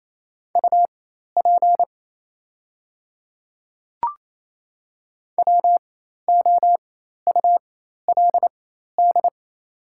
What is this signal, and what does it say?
Morse code, heard as a keyed tone, about 28 wpm, spaced as for 13 wpm, 690 Hz. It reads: UP WOULD
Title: 4x Character Spacing - Top 200 Words - 28wpm